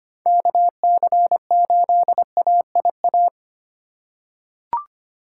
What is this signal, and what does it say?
Morse code, heard as a keyed tone, about 25 wpm, 700 Hz. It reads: KC8AIA
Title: USA Call Signs - 4x Character Spacing - 25wpm